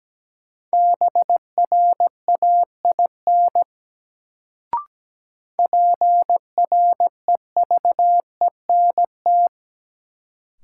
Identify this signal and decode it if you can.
Morse code, heard as a keyed tone, about 17 wpm, 700 Hz. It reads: BRAIN PREVENT
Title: Sentences from Top 1100 Words - New Word Drill Alt - 17wpm